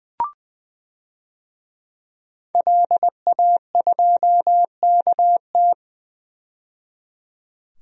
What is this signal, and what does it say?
Morse code, heard as a keyed tone, about 20 wpm, 700 Hz. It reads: LA2KT
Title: DXCC-100 Call Signs - 5 Characters - 20wpm